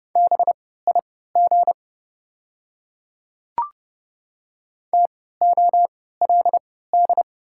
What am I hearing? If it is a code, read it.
Morse code, heard as a keyed tone, about 30 wpm, spaced as for 17 wpm, 690 Hz. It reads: BIG TOLD
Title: 3x Character Spacing - Top 500 Words - 30wpm